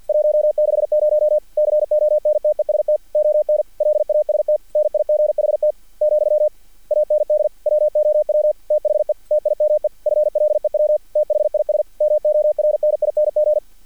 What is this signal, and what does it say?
Morse code, heard as a keyed tone, about 50 wpm, 600 Hz. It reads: 160 CONTEST ON LAST NIGHT , AND WOW THE TIME FLEW THIS MORNING
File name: archived / CRSnet